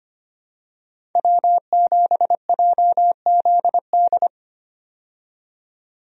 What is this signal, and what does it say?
Morse code, heard as a keyed tone, about 25 wpm, 700 Hz. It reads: W7JZD
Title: USA Call Signs - 5 Characters - 25wpm